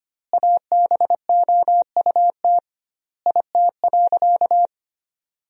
Morse code, heard as a keyed tone, about 25 wpm, 700 Hz. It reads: ABOUT IT.
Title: 2x Word-Spacing - Sentences from Top 200 Words - 25wpm